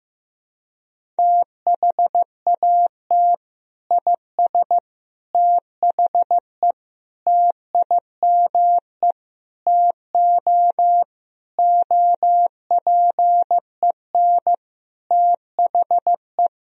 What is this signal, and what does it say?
Morse code, heard as a keyed tone, about 15 wpm, 700 Hz. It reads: THAT IS THE TIME TO OPEN THE
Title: Sentences from Top 1000 Words - Review Part 1 - 15wpm